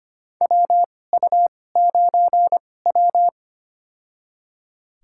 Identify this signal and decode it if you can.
Morse code, heard as a keyed tone, about 25 wpm, spaced as for 18 wpm, 700 Hz. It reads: WU9W
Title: USA Call Signs - 4 Characters - 2x Character Spacing - 25wpm